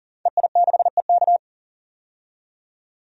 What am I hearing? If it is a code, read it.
Morse code, heard as a keyed tone, about 40 wpm, 700 Hz. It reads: EI6EK
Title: DXCC-100 Call Signs - 5 Characters - 40wpm